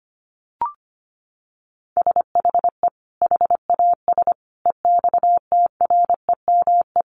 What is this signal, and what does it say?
Morse code, heard as a keyed tone, about 25 wpm, 700 Hz. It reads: SHE HAS EXTREME
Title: Sentences from Top 1300 Words 25wpm